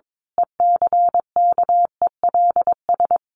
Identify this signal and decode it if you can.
Morse code, heard as a keyed tone, about 22 wpm, 700 Hz. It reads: ECKELS
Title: A Sound Of Thunder 22wpm